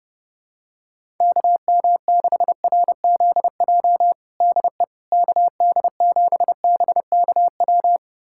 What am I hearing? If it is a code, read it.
Morse code, heard as a keyed tone, about 30 wpm, 700 Hz. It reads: KM6RZJ DE KD7BKW
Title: USA Callsign DE Callsign - 6 Characters - 30wpm